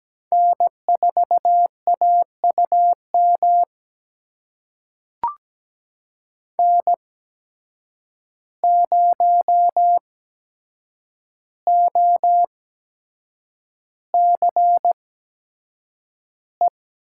Morse code, heard as a keyed tone, about 17 wpm, spaced as for 7 wpm, 700 Hz. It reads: N4AUM N0OCE
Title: USA Call Signs - 5 Characters - 8x Character Spacing - 17wpm